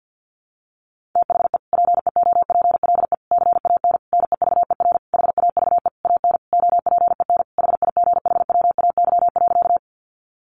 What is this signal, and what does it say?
Morse code, heard as a keyed tone, about 50 wpm, 700 Hz. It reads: THE PEOPLE CAN NEVER HAVE AN OPEN HIGHWAY.
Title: Sentences from Top 1400 Words 50wpm